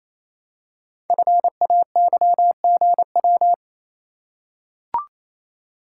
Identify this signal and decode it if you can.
Morse code, heard as a keyed tone, about 27 wpm, 700 Hz. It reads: FAYGW
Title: Code Groups - 5 Letters - 28wpm